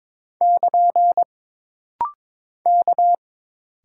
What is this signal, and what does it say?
Morse code, heard as a keyed tone, about 22 wpm, 700 Hz. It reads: <KN> K
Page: This rapid-fire timing is geared to push your ICR (Instant Character Recognition) skills.